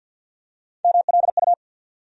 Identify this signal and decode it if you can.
Morse code, heard as a keyed tone, about 50 wpm, 700 Hz. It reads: MLU
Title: USA Call Sign Postfixes - 3 Letters 50wpm